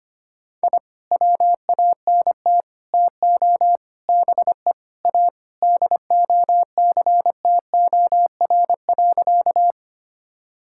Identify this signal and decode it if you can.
Morse code, heard as a keyed tone, about 25 wpm, 700 Hz. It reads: I WANT TO BE A DOCTOR.
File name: practice / 25wpm / Sentences with 3-Word Phrases - 25wpm